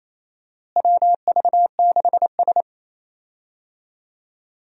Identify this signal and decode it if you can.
Morse code, heard as a keyed tone, about 28 wpm, 700 Hz. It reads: WV6S
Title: USA Call Signs - 4 Characters - Repeated 3x - 28wpm